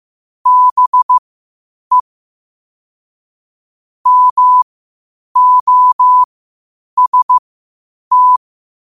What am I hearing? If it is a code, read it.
Morse code, heard as a keyed tone, about 15 wpm, spaced as for 8 wpm, 1000 Hz. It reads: BE MOST
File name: qotdfiles / 8wpm